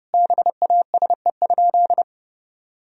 Morse code, heard as a keyed tone, about 30 wpm, 700 Hz. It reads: BASE?
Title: Sentences from Top 1000 Words - Review Part 1 - 30wpm